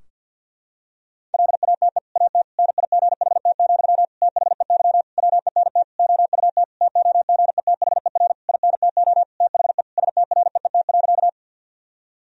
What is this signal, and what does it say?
Morse code, heard as a keyed tone, about 50 wpm, 700 Hz. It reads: LATE AT NIGHT, THEY WENT OUT TOGETHER INTO THE STREET.
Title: Sentences from Top 1000 Words - Review Part 3 - 50wpm